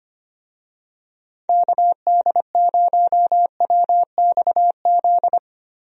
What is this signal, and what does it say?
Morse code, heard as a keyed tone, about 25 wpm, 700 Hz. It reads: KD0WXZ
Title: USA Call Signs - 6 Characters - 25wpm